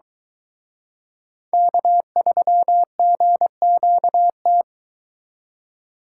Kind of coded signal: Morse code

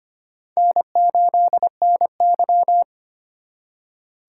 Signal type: Morse code

